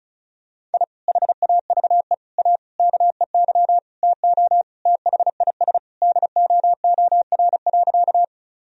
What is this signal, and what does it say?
Morse code, heard as a keyed tone, about 36 wpm, 700 Hz. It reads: I HAVE A KEY TO THIS DOOR.
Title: Sentences from Top 1000 Words - Review Part 4 - 35wpm